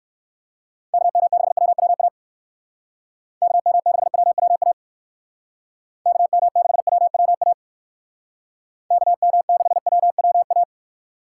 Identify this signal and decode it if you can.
Morse code, heard as a keyed tone, about 59 wpm, 700 Hz. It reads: KM6WWA KM6WWA KM6WWA KM6WWA